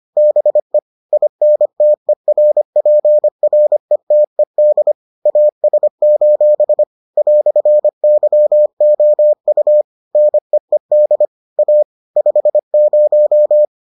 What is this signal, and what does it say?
Morse code, heard as a keyed tone, about 25 wpm, 600 Hz. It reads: BE INTERPRETED AS: "YOU NEED A 50